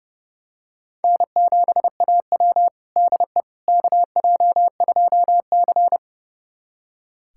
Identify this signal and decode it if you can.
Morse code, heard as a keyed tone, about 30 wpm, 700 Hz. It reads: N7AW DE KJ2C
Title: USA Callsign DE Callsign - 4 Characters - 30wpm